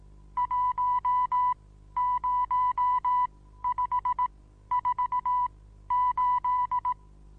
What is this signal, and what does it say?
Morse code, heard as a keyed tone, about 18 wpm, spaced as for 12 wpm, 990 Hz. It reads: 10548
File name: Onlinelehrgang / morsen / DJ4UF Morsen25